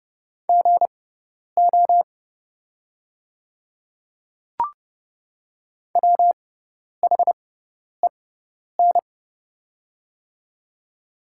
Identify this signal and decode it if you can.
Morse code, heard as a keyed tone, about 30 wpm, spaced as for 10 wpm, 700 Hz. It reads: GO WHEN
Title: 6x Character Spacing - Top 100 Words - 30wpm